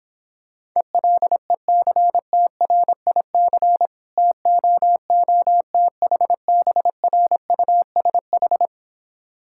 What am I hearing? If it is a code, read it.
Morse code, heard as a keyed tone, about 26 wpm, 700 Hz. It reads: ELECTRIC TOOTHBRUSH